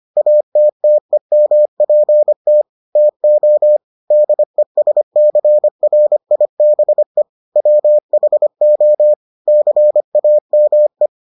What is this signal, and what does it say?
Morse code, heard as a keyed tone, about 25 wpm, 600 Hz. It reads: ATTEMPT TO DESCRIBE WHO CAME